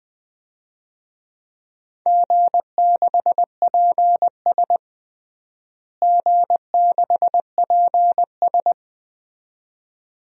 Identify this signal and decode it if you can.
Morse code, heard as a keyed tone, about 20 wpm, 700 Hz. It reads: G6PS G6PS
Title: DXCC-100 Call Signs - 4 Characters - Repeated 2x - 20wpm